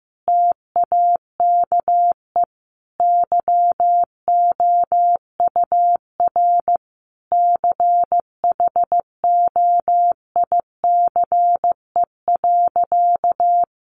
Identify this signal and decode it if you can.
Morse code, heard as a keyed tone, about 15 wpm, 700 Hz. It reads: TAKE YOUR CHOICE.
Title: Sentences from Top 900 Words 15wpm